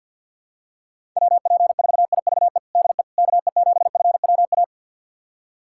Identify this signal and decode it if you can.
Morse code, heard as a keyed tone, about 49 wpm, 700 Hz. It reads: WP4IVE DE KE7FWA
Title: USA Callsign DE Callsign 50wpm